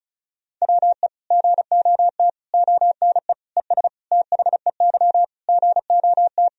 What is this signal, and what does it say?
Morse code, heard as a keyed tone, about 36 wpm, 700 Hz. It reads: WE GOT ONE ES THEY GOT
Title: Sentences from Top 300 Words and Abbreviations 35wpm